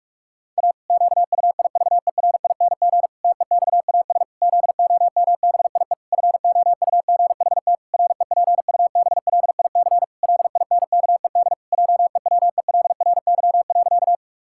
Morse code, heard as a keyed tone, about 46 wpm, 700 Hz. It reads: A QUIVERING TEXAS ZOMBIE FOUGHT REPUBLIC LINKED JEWELRY.
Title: Pangram Challenge v2 - 45wpm